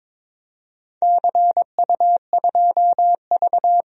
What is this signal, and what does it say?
Morse code, heard as a keyed tone, about 21 wpm, 700 Hz. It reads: CU2V
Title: DXCC-100 Call Signs - 4 Characters - 22wpm